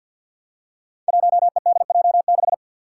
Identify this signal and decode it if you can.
Morse code, heard as a keyed tone, about 51 wpm, 700 Hz. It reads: 1EDJ6
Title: Five Challenging Letter-Number Combinations 50wpm